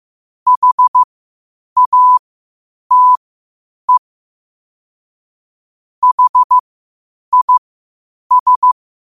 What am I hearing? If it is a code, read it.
Morse code, heard as a keyed tone, about 15 wpm, spaced as for 8 wpm, 1000 Hz. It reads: HATE HIS
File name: qotdfiles / 8wpm